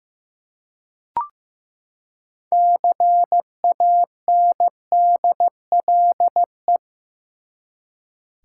Morse code, heard as a keyed tone, about 15 wpm, 700 Hz. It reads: CANDLE
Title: Six Letter Words - 15wpm